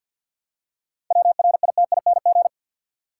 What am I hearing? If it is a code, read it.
Morse code, heard as a keyed tone, about 52 wpm, 700 Hz. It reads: WRITING
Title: Seven Letter Words - 50wpm